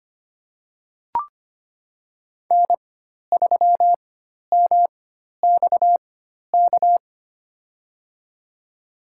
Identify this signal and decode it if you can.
Morse code, heard as a keyed tone, about 25 wpm, spaced as for 12 wpm, 700 Hz. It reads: N3MXK
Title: USA Call Signs - 5 Characters - 4x Character Spacing - 25wpm